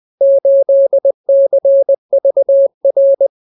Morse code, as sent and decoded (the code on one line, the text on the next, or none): ---.. -.-. ...- .-.
8CVR